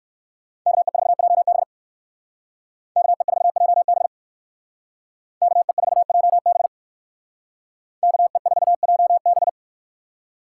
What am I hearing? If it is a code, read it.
Morse code, heard as a keyed tone, about 63 wpm, 700 Hz. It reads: KE4JB KE4JB KE4JB KE4JB